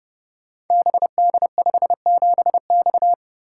Morse code, heard as a keyed tone, about 30 wpm, 700 Hz. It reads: BD57X
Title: Code Groups - 5 Letter:Numbers - 30wpm